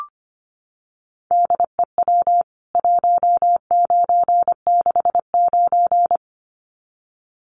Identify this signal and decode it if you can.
Morse code, heard as a keyed tone, about 25 wpm, 700 Hz. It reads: DEW 1969
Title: CWT Exchange 25wpm